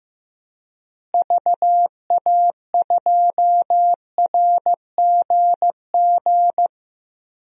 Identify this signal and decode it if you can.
Morse code, heard as a keyed tone, about 15 wpm, 700 Hz. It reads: VA2RGG
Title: DXCC-100 Call Signs - 6 Characters - Repeated 2x - 15wpm